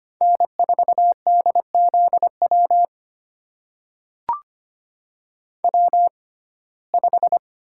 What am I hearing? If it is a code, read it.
Morse code, heard as a keyed tone, about 25 wpm, spaced as for 18 wpm, 700 Hz. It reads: N4DZW W5
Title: USA Call Signs - 5 Characters - 6x Character Spacing - 25wpm